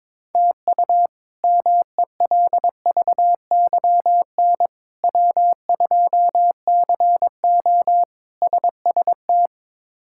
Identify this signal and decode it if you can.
Morse code, heard as a keyed tone, about 22 wpm, 700 Hz. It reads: TU MELVYN W2CO SST
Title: K1USN SST QSO Practice - 22wpm